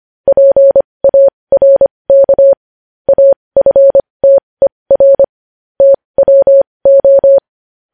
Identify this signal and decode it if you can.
Morse code, heard as a keyed tone, about 25 wpm, spaced as for 20 wpm, 550 Hz. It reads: PARK AFTER TWO
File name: podcast / News headlines-1x-25x20